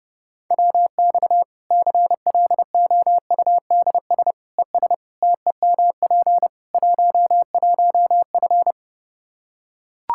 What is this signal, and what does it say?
Morse code, heard as a keyed tone, about 30 wpm, 700 Hz. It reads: WX CLOUDS ES TEMP 11F
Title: Practice QSO - Exchange 2 - Info - 30wpm